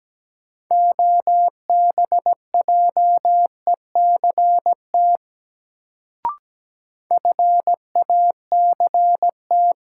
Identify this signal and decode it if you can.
Morse code, heard as a keyed tone, about 17 wpm, 700 Hz. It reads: OBJECT FACT
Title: Sentences from Top 500 Words - New Word Drill - 17wpm